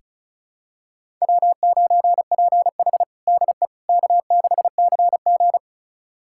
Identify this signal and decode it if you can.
Morse code, heard as a keyed tone, about 34 wpm, 700 Hz. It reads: W9PH DE K6CG
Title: USA Callsign DE Callsign - 4 Characters - 35wpm